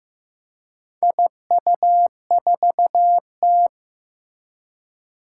Morse code, heard as a keyed tone, about 15 wpm, 700 Hz. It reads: IU4T
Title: Four Challenging Letter-Number Combinations 15wpm